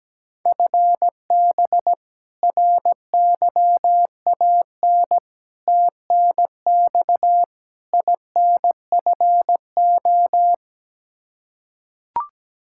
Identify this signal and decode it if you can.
Morse code, heard as a keyed tone, about 17 wpm, 700 Hz. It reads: FB RYAN TNX INFO
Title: Practice QSO - Exchange 3 - More Info - 17wpm